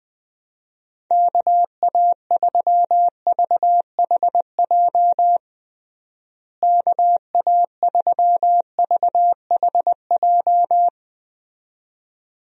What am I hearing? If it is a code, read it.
Morse code, heard as a keyed tone, about 20 wpm, 700 Hz. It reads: KA3VHJ KA3VHJ
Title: USA Call Signs - 6 Characters - Repeated 2x - 20wpm